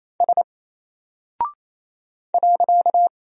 Morse code, heard as a keyed tone, about 27 wpm, 700 Hz. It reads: S .
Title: Single Character (Letter:Number:Punctuation:Prosign) 28wpm